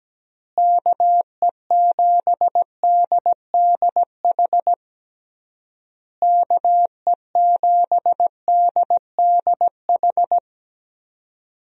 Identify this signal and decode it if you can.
Morse code, heard as a keyed tone, about 17 wpm, 700 Hz. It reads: KE7DDH KE7DDH
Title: USA Call Signs - 6 Characters - Repeated 3x - 17wpm